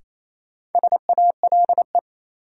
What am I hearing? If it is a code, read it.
Morse code, heard as a keyed tone, about 28 wpm, 700 Hz. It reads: SALE
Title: Sentences from Top 1400 Words - New Word Drill Alt - 28wpm